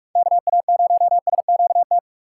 Morse code, heard as a keyed tone, about 44 wpm, 700 Hz. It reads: KA0SQT
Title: USA Call Signs - 6 Characters - 6x Character Spacing - 45wpm